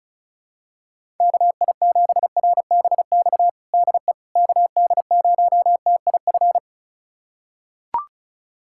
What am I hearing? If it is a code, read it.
Morse code, heard as a keyed tone, about 36 wpm, 700 Hz. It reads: KI7RBX DE KD0TIF
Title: USA Callsign DE Callsign - 6 Characters - 35wpm